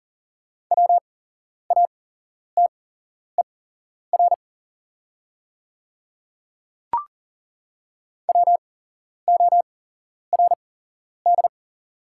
Morse code, heard as a keyed tone, about 40 wpm, spaced as for 11 wpm, 700 Hz. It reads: WATER WORD
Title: 8x Character Spacing - Top 200 Words - 40wpm